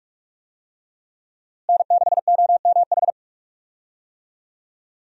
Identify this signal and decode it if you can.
Morse code, heard as a keyed tone, about 45 wpm, 700 Hz. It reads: N6OMH
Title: USA Call Signs - 5 Characters - Repeated 3x - 45wpm